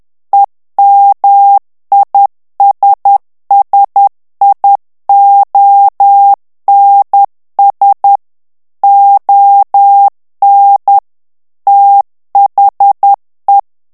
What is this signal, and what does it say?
Morse code, heard as a keyed tone, about 10 wpm, 800 Hz. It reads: EMISSIONS ON THE
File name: ANGLAIS texte clair 3 vitesse 10 mots minute